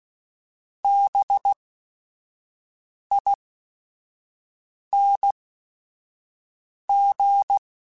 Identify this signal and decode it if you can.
Morse code, heard as a keyed tone, about 16 wpm, spaced as for 5 wpm, 800 Hz. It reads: BING
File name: Special INGs / Special INGs-mp3s / Sngl Ing @5